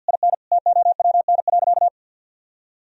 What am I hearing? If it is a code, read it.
Morse code, heard as a keyed tone, about 46 wpm, 700 Hz. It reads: IN TOWN.
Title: Sentences from Top 1000 Words - Review Part 4 - 50wpm